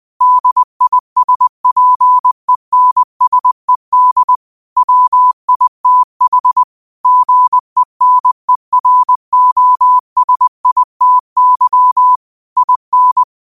Morse code, heard as a keyed tone, about 20 wpm, 1000 Hz. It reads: DISPENSED WITH GENEROSITY IN
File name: qotdfiles / 20wpm